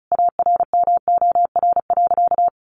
Morse code, heard as a keyed tone, about 34 wpm, 700 Hz. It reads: ARMOR.
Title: A Sound Of Thunder 35wpm